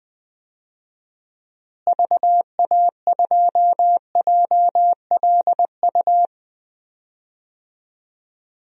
Morse code, heard as a keyed tone, about 21 wpm, 700 Hz. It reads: VA2JLU
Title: DXCC-100 Call Signs - 6 Characters - Repeated 3x - 20wpm